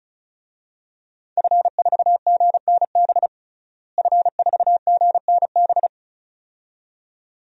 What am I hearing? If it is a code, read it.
Morse code, heard as a keyed tone, about 34 wpm, 700 Hz. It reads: F4GNB F4GNB
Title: DXCC-100 Call Signs - 5 Characters - Repeated 2x - 35wpm